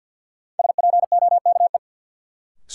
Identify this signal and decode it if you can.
Morse code, heard as a keyed tone, about 48 wpm, 700 Hz. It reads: SPOKE
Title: Sentences from Top 1200 Words - New Word Drill Alt - 50wpm